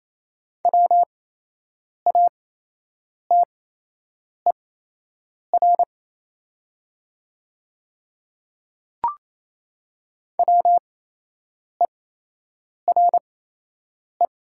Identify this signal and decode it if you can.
Morse code, heard as a keyed tone, about 27 wpm, spaced as for 8 wpm, 690 Hz. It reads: WATER WERE
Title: 8x Character Spacing - Top 300 Words - 28wpm